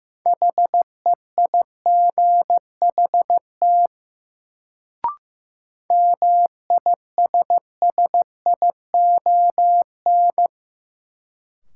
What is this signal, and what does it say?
Morse code, heard as a keyed tone, about 15 wpm, 700 Hz. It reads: HEIGHT MISSION
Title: Sentences from Top 1600 Words - New Word Drill - 15wpm